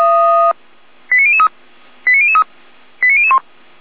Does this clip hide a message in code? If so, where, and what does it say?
Yes: five-tone selective call (ZVEI) from 1.1 to 3.4 s, heard as quick runs of five tones, tones of about 70 ms. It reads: 89003 89003 89001